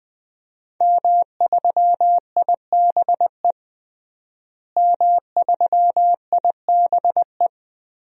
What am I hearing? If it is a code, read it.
Morse code, heard as a keyed tone, about 20 wpm, 700 Hz. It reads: M3IBE M3IBE
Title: DXCC-100 Call Signs - 5 Characters - Repeated 3x - 20wpm